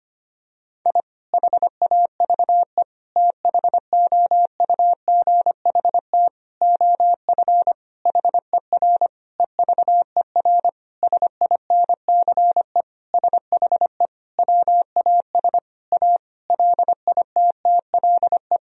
Morse code, heard as a keyed tone, about 25 wpm, 700 Hz. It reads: I HAVE THOUGHT OF HER EVER SINCE SHE WAS A LITTLE
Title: Sentences from Top 1000 Words - Review Part 1 - 25wpm